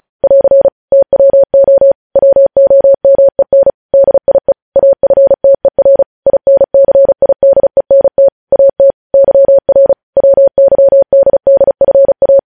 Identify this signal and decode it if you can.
Morse code, heard as a keyed tone, about 36 wpm, 550 Hz. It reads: <AR> TWO WOMEN DIE AFTER INCIDENT AT YR WYDDFA
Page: news headlines in morse code at 35 wpm